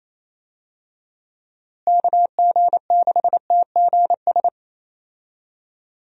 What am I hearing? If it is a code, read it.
Morse code, heard as a keyed tone, about 28 wpm, 700 Hz. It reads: KG6TGS
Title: USA Call Signs - 6 Characters - Repeated 2x - 28wpm